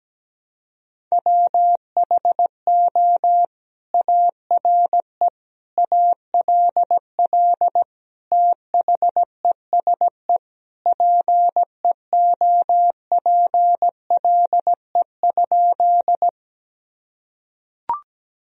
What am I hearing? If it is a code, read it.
Morse code, heard as a keyed tone, about 17 wpm, 700 Hz. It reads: WHO ARE ALL THESE PEOPLE?
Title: Sentences from Top 1000 Words - Review Part 1 - 17wpm